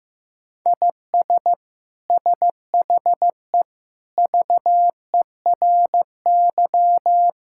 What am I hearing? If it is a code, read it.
Morse code, heard as a keyed tone, about 15 wpm, 700 Hz. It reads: IS SHE VERY
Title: Sentences from Top 1000 Words - Review Part 5 - 15wpm